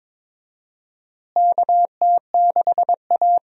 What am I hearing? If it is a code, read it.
Morse code, heard as a keyed tone, about 22 wpm, 700 Hz. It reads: KT6A
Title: USA Call Signs - 4 Characters - Repeated 2x - 22wpm